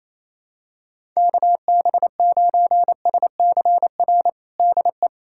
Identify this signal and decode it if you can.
Morse code, heard as a keyed tone, about 28 wpm, 700 Hz. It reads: KB9SCR DE
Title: USA Callsign DE Callsign - 6 Characters - 28wpm